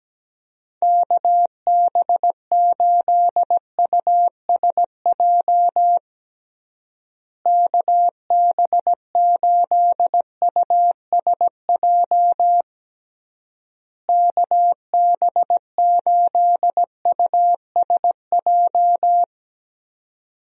Morse code, heard as a keyed tone, about 17 wpm, 700 Hz. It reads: KB8USJ KB8USJ KB8USJ
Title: USA Call Signs - 6 Characters - Repeated 3x - 17wpm